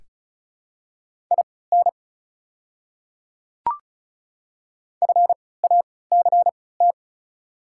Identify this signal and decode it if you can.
Morse code, heard as a keyed tone, about 34 wpm, spaced as for 20 wpm, 690 Hz. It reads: IN FACT
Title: 3x Character Spacing - Top 500 Words - 35wpm